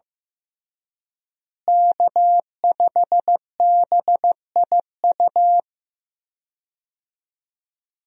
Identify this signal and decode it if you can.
Morse code, heard as a keyed tone, about 15 wpm, 700 Hz. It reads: K5BIU
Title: USA Call Signs - 5 Characters - Repeated 2x - 15wpm